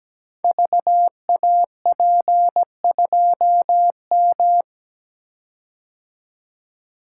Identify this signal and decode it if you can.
Morse code, heard as a keyed tone, about 17 wpm, 700 Hz. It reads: VAP2M
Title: Code Groups - 5 Letter:Numbers - 17wpm